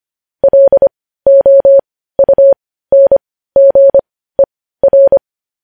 Morse code, heard as a keyed tone, about 25 wpm, spaced as for 15 wpm, 550 Hz. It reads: LOUNGER
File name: podcast / News headlines-1x-25x15